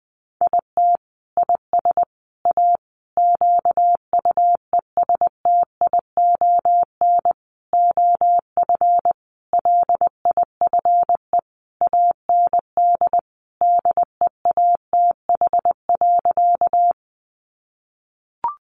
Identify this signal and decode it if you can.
Morse code, heard as a keyed tone, about 20 wpm, 700 Hz. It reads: IT IS A QUESTION OF LIFE AND DEATH.
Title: Sentences from Top 1100 Words 20wpm